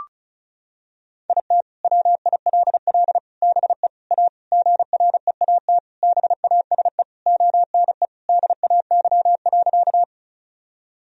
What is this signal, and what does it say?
Morse code, heard as a keyed tone, about 36 wpm, 700 Hz. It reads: IT WILL BE A GREAT BASE ONE DAY.
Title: Sentences from Top 1000 Words - Review Part 1 - 35wpm